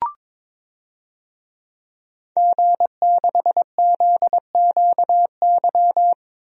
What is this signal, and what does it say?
Morse code, heard as a keyed tone, about 22 wpm, 700 Hz. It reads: G6ZQY
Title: DXCC-100 Call Signs - 5 Characters - 22wpm